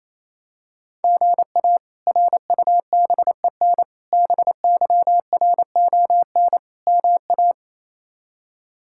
Morse code, heard as a keyed tone, about 28 wpm, 700 Hz. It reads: GA RUBEN BYRON MA
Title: K1USN SST QSO Practice - 28wpm